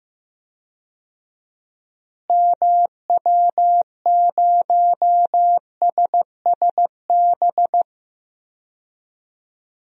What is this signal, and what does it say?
Morse code, heard as a keyed tone, about 16 wpm, 700 Hz. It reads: MW0SSB
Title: DXCC-100 Call Signs - 6 Characters - Repeated 3x - 15wpm